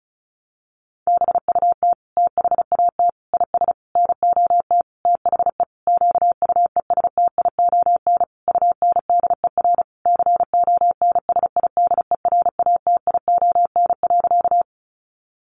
Morse code, heard as a keyed tone, about 34 wpm, 700 Hz. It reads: BUT THAT IS NOT THE QUESTION UNDER CONSIDERATION.
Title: Sentences from Top 1500 Words 35wpm